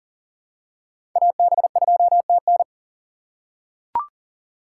Morse code, heard as a keyed tone, about 40 wpm, 700 Hz. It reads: AB2TN
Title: USA Call Signs - 5 Characters - 2x Character Spacing - 40wpm